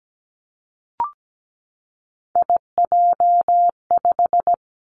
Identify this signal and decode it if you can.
Morse code, heard as a keyed tone, about 17 wpm, 700 Hz. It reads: IJ5
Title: Three Challenging Letter-Number Combinations 17wpm